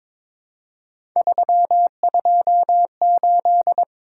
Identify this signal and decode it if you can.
Morse code, heard as a keyed tone, about 22 wpm, 700 Hz. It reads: 328
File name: practice / 22wpm / Three Letter-Number Combinations - 22wpm